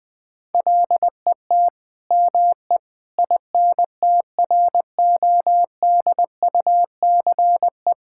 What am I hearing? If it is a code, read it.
Morse code, heard as a keyed tone, about 20 wpm, 700 Hz. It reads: LET ME INTRODUCE
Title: Sentences from Top 1600 Words 20wpm